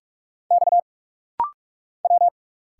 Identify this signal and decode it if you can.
Morse code, heard as a keyed tone, about 46 wpm, 700 Hz. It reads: X W
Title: Single Letters - Rapid-Fire 45wpm